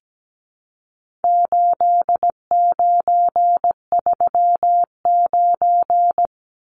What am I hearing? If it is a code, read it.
Morse code, heard as a keyed tone, about 17 wpm, 700 Hz. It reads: 8939
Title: Numbers - 4-Digit 17wpm